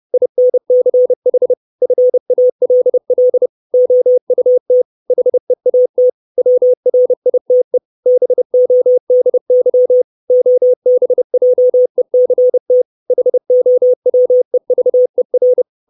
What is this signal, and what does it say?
Morse code, heard as a keyed tone, about 30 wpm, 500 Hz. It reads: INCH FALL OUT HEAT WRITE BODY OBJECT HOWEVER
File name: 632 Words 30WPM